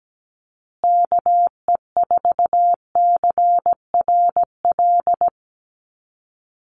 Morse code, heard as a keyed tone, about 17 wpm, 700 Hz. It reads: KE4CRL
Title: USA Call Signs Repeated 3x 17wpm